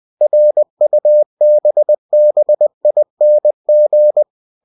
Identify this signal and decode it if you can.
Morse code, heard as a keyed tone, about 20 wpm, 600 Hz. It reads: RUBBING
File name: cw-books / winnie-the-pooh / winnie-20wpm-page-0045